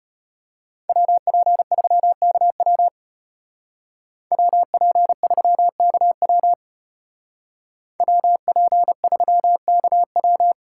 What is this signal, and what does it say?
Morse code, heard as a keyed tone, about 35 wpm, 700 Hz. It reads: WP3KW WP3KW WP3KW